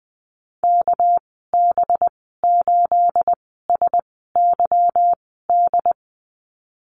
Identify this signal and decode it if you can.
Morse code, heard as a keyed tone, about 19 wpm, spaced as for 14 wpm, 700 Hz. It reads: KB8SYD
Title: 238 Lesson 38g - Letter q - Call Signs 20wpm